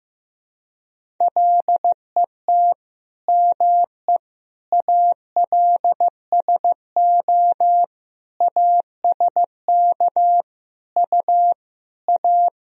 Morse code, heard as a keyed tone, about 15 wpm, 700 Hz. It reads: LET ME ALSO ASK U A
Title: Sentences from Top 400 Words and Abbreviations 15wpm